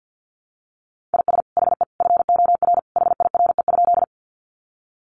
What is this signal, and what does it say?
Morse code, heard as a keyed tone, about 50 wpm, 700 Hz. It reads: IS HE FOR HIRE?
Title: Sentences from Top 1400 Words 50wpm